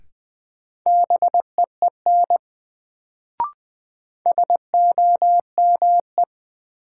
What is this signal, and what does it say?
Morse code, heard as a keyed tone, about 20 wpm, 700 Hz. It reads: BEEN SOME
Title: Sets of 2 Words - New Word Drill - 20wpm